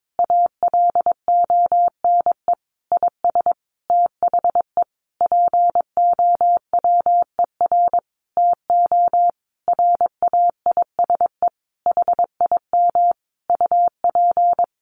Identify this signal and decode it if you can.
Morse code, heard as a keyed tone, about 22 wpm, 700 Hz. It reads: ALONE IS THE POWER TO RAISE HIM UP
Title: Sentences from Top 1400 Words 22wpm